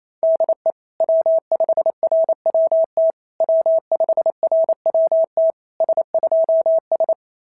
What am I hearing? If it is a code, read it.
Morse code, heard as a keyed tone, about 25 wpm, 650 Hz. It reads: DE W5RWT W5RWT S2S
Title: SOTA QSO Practice 28wpm